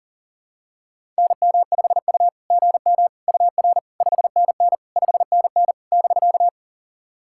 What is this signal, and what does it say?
Morse code, heard as a keyed tone, about 40 wpm, 690 Hz. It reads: NM5U GM UR 5NN 5NN <BK>
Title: SOTA QSO Practice 40wpm